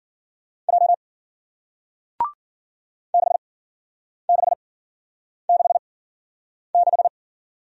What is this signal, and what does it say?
Morse code, heard as a keyed tone, about 51 wpm, spaced as for 9 wpm, 700 Hz. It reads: J 6666